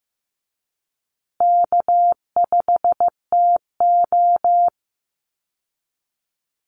Morse code, heard as a keyed tone, about 15 wpm, 700 Hz. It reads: K5TO
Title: Easy Call Signs Repeated 2x - 15wpm